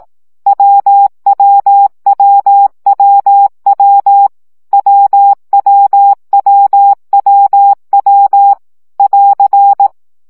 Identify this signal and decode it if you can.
Morse code, heard as a keyed tone, about 18 wpm, 800 Hz. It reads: WWWWW WWWWW <AR>
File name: CW gyakorlas w 15 baud